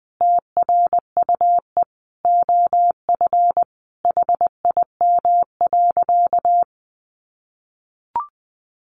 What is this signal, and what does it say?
Morse code, heard as a keyed tone, about 20 wpm, 700 Hz. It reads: TRUE OF HIM.
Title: Sentences from Top 700 Words 20wpm